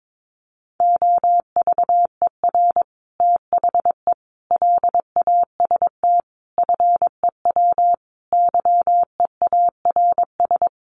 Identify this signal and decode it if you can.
Morse code, heard as a keyed tone, about 22 wpm, 700 Hz. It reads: OVER THE LAST FEW YEARS
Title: Sets of 5 Words Encore 22wpm